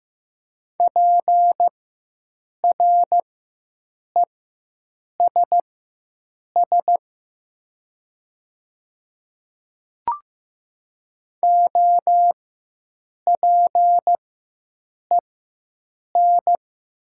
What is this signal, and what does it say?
Morse code, heard as a keyed tone, about 15 wpm, spaced as for 7 wpm, 700 Hz. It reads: PRESS OPEN
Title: 4x Character Spacing - Top 300 Words - 15wpm